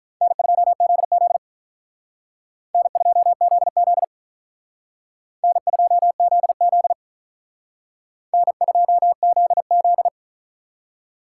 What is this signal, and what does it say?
Morse code, heard as a keyed tone, about 45 wpm, 700 Hz. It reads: N2ZZ N2ZZ N2ZZ N2ZZ